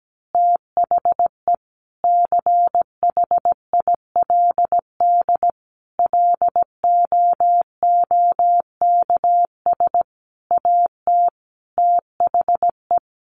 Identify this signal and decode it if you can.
Morse code, heard as a keyed tone, about 17 wpm, 700 Hz. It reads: THE CHILD LOOKS AT THE
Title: Additional Easy Phrases 17wpm